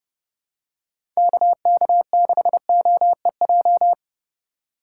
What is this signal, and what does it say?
Morse code, heard as a keyed tone, about 30 wpm, 700 Hz. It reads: KK6OEJ
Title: USA Call Signs - 6 Characters - 3x Character Spacing - 30wpm